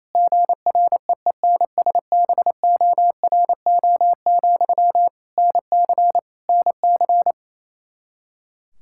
Morse code, heard as a keyed tone, about 28 wpm, 700 Hz. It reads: GREENSBORO, NC NC
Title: Practice QSO - Exchange 1 - Beginning - 28wpm